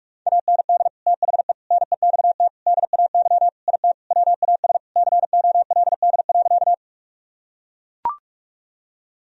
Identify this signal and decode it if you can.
Morse code, heard as a keyed tone, about 44 wpm, 700 Hz. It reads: AND THE NEXT DAY IT WAS COLD.
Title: Sentences from Top 1000 Words - Review Part 1 - 45wpm